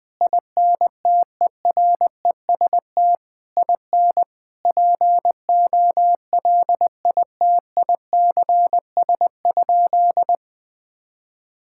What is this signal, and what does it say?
Morse code, heard as a keyed tone, about 20 wpm, 700 Hz. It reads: INTEREST IN POLITICS?
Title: Sentences from Top 1000 Words - Review Part 6 - 20wpm